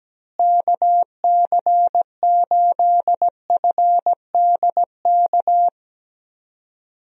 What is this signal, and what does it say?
Morse code, heard as a keyed tone, about 17 wpm, 700 Hz. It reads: KC8FDK
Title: USA Call Signs - 6 Characters - Repeated 3x - 17wpm